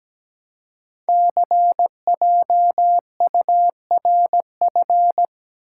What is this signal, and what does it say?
Morse code, heard as a keyed tone, about 17 wpm, 700 Hz. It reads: CJURF
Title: Code Groups - 5 Letters - 17wpm